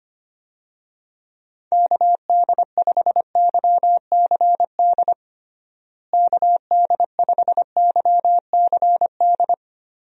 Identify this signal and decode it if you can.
Morse code, heard as a keyed tone, about 25 wpm, 700 Hz. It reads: KD5YCD KD5YCD
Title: USA Call Signs - 6 Characters - Repeated 2x - 25wpm